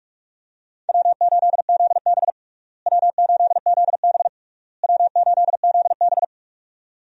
Morse code, heard as a keyed tone, about 41 wpm, 700 Hz. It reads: W8ZB W8ZB W8ZB
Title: USA Call Signs - 4 Characters - Repeated 3x - 45wpm